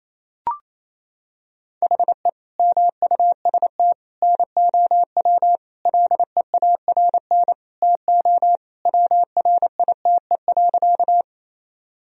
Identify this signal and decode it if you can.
Morse code, heard as a keyed tone, about 27 wpm, 700 Hz. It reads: HE MUST NOW LEARN TO WRITE.
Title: Sentences from Top 1000 Words - Review Part 1 - 28wpm